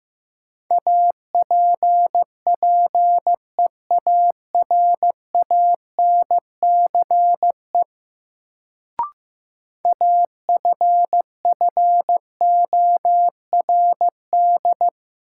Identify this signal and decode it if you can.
Morse code, heard as a keyed tone, about 15 wpm, 700 Hz. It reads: APPEARANCE AFFORD
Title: Sentences from Top 1100 Words - New Word Drill Alt - 15wpm